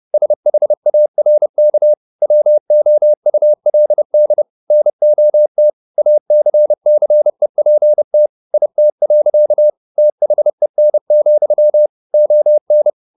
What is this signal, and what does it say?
Morse code, heard as a keyed tone, about 30 wpm, 600 Hz. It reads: SHARK WOULD NOT ACCEPT IT. THEN, ON